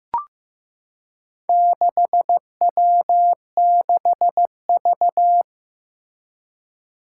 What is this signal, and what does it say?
Morse code, heard as a keyed tone, about 15 wpm, 700 Hz. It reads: 6W6V
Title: Four Challenging Letter-Number Combinations 15wpm